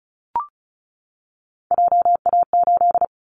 Morse code, heard as a keyed tone, about 36 wpm, 700 Hz. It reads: JA8
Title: Callsign Prefixes for Top 25 DXCC - 35wpm